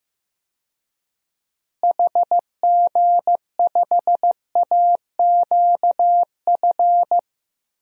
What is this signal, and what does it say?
Morse code, heard as a keyed tone, about 15 wpm, 700 Hz. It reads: HG5AQF
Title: DXCC-100 Call Signs - 6 Characters - Repeated 3x - 15wpm